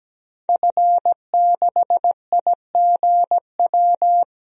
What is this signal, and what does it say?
Morse code, heard as a keyed tone, about 17 wpm, 700 Hz. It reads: F6IGW
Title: DXCC-100 Call Signs - 5 Characters - 17wpm